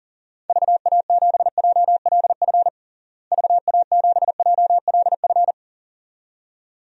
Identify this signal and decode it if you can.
Morse code, heard as a keyed tone, about 40 wpm, 700 Hz. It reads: VA7JLF VA7JLF
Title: DXCC-100 Call Signs - 6 Characters - Repeated 2x - 40wpm